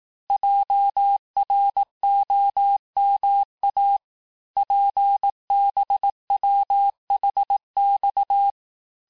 Morse code, heard as a keyed tone, about 18 wpm, 800 Hz. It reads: JROMA PBWHX
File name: cours-du-lundi files / cours-1852-texte-chiffre-vitesse-17MPM